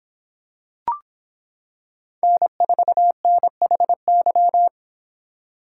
Morse code, heard as a keyed tone, about 25 wpm, 700 Hz. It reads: N4NHY